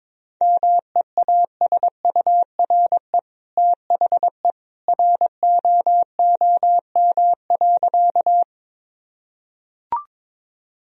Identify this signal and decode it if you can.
Morse code, heard as a keyed tone, about 22 wpm, 700 Hz. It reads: MEASURE THE ROOM.
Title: Sentences from Top 1000 Words - Review Part 2 - 22wpm